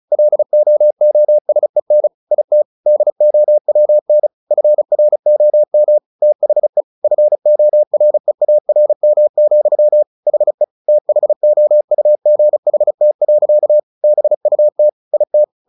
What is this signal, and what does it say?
Morse code, heard as a keyed tone, about 35 wpm, 600 Hz. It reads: LOOSEN IT DOWN FROM THE FOREARM, HE THOUGHT. BUT IT
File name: the old man and the sea / the old man and the sea 1 35wpm